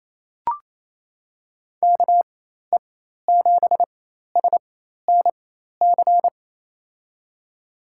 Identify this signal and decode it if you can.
Morse code, heard as a keyed tone, about 28 wpm, spaced as for 13 wpm, 700 Hz. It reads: KE7SNC
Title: USA Call Signs - 4x Character Spacing - 28wpm